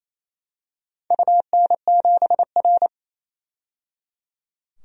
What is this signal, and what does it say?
Morse code, heard as a keyed tone, about 27 wpm, 700 Hz. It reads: UN7R
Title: DXCC-100 Call Signs - 4 Characters - 28wpm